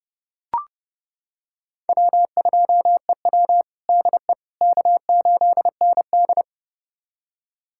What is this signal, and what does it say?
Morse code, heard as a keyed tone, about 32 wpm, 700 Hz. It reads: W2EW DE K8ND
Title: USA Callsign DE Callsign - 4 Characters - 30wpm